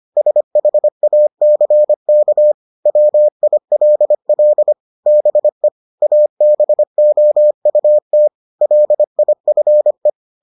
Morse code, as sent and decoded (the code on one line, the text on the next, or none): ... .... .- -.-. -.- / .-- .. .-.. .-.. / -... / .- -... --- ..- - / .-.. .. ..-. .
SHACK WILL BE ABOUT LIFE